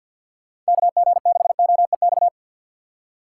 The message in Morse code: KG6OEX

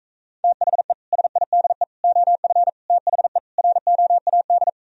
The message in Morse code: THE SIDE OF THE ROAD

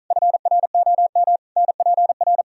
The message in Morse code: FROM NPR

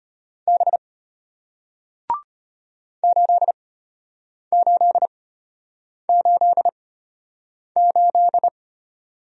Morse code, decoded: B 8888

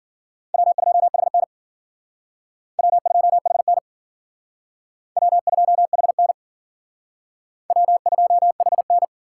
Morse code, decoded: W2HN W2HN W2HN W2HN

